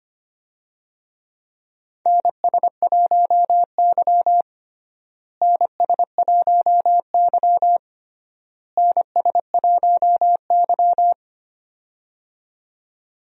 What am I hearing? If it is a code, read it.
Morse code, heard as a keyed tone, about 25 wpm, 700 Hz. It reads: NS1Y NS1Y NS1Y
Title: USA Call Signs - 4 Characters - Repeated 3x - 25wpm